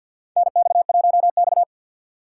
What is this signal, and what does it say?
Morse code, heard as a keyed tone, about 49 wpm, 700 Hz. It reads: NX1X
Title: USA Call Signs - 4x Character Spacing - 50wpm